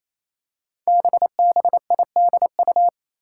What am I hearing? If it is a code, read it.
Morse code, heard as a keyed tone, about 28 wpm, 700 Hz. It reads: BBIDU
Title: Five Challenging Letter-Number Combinations 28wpm